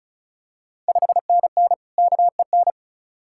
Morse code, heard as a keyed tone, about 36 wpm, 700 Hz. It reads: 5NN KEN
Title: Ontario QSO Party Exchange 35wpm